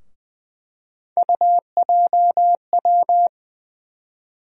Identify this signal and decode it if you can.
Morse code, heard as a keyed tone, about 21 wpm, 700 Hz. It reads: UJW
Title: USA Call Sign Postfixes - 3 Letters 20wpm